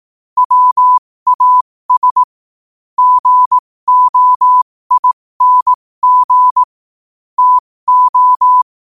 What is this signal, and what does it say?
Morse code, heard as a keyed tone, about 18 wpm, spaced as for 15 wpm, 1000 Hz. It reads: WAS GOING TO
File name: qotdfiles / 15wpm